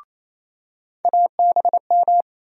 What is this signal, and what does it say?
Morse code, heard as a keyed tone, about 28 wpm, 700 Hz. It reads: ABM
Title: USA Call Sign Postfixes - 3 Letters 28wpm